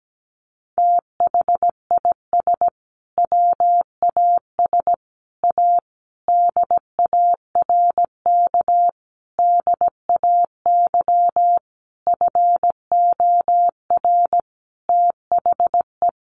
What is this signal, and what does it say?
Morse code, heard as a keyed tone, about 17 wpm, 700 Hz. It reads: THIS WAS A DARK DAY FOR THE
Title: Sentences from Top 1300 Words 17wpm